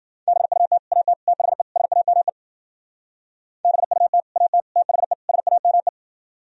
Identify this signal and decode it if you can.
Morse code, heard as a keyed tone, about 50 wpm, 700 Hz. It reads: BUT AT THE SAME BUT AT THE SAME
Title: Speed-Racing - Sets of 4 Words 40wpm